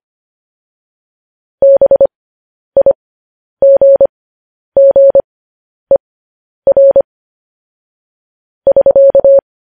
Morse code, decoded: BIGGER <SK>

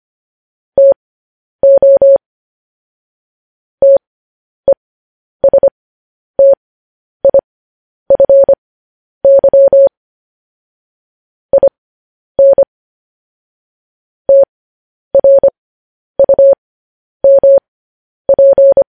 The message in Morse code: TO TESTIFY IN TRUMP

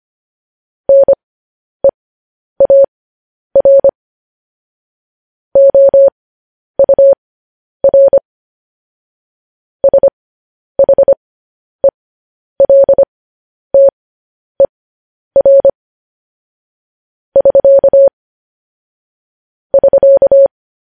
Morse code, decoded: NEAR OUR SHELTER <SK> <SK>

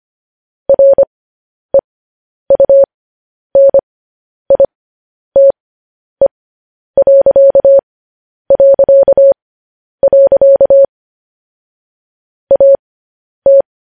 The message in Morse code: REUNITE... AT

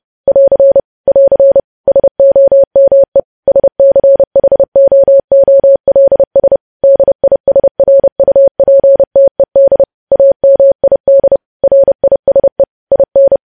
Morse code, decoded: <AR> <AR> SOME SCHOOLS DISRUPTED AMID RISE IN

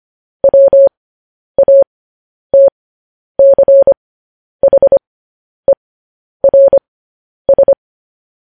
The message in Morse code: WATCHERS

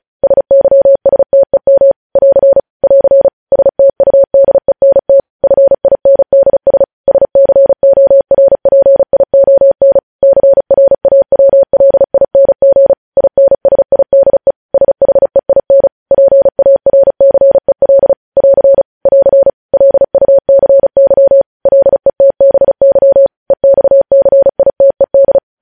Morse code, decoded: SYSTEM <AR> <AR> STUDENT FINDS SCORPION CRAWLING INSIDE SHEIN PARCEL <AR> <AR> LUCY LETBY EXCITED

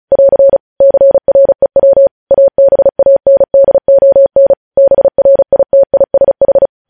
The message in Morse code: <AR> CREW ABANDON BRITISH

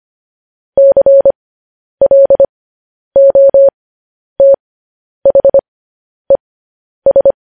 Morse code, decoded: CLOTHES

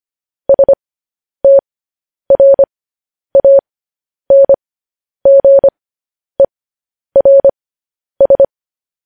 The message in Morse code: STRANGERS

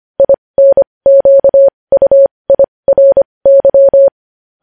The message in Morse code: INQUIRY

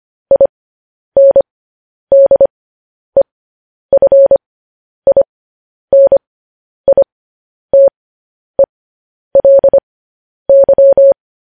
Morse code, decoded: INDEFINITELY

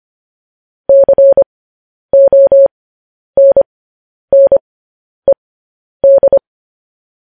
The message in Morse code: CONNED

